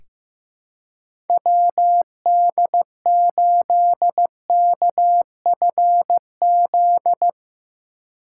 Morse code, heard as a keyed tone, about 15 wpm, 700 Hz. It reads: WD8KFZ
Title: USA Call Signs - 3x Character Spacing - 15wpm